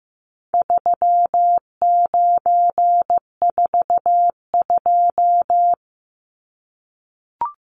Morse code, heard as a keyed tone, about 15 wpm, 700 Hz. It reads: 3942
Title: Numbers - 4-Digit 15wpm